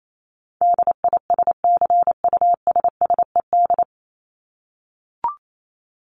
Morse code, heard as a keyed tone, about 28 wpm, 700 Hz. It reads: DISCUSSED
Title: ED Suffix - Top 100 Words with Base Words 28wpm